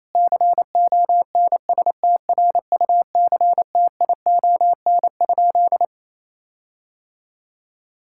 Morse code, heard as a keyed tone, about 28 wpm, 700 Hz. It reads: CONSTRUCTION?
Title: Sentences from Top 1600 Words 28wpm